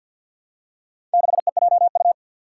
Morse code, heard as a keyed tone, about 51 wpm, 700 Hz. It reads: 6EJU
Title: Four Challenging Letter-Number Combinations 50wpm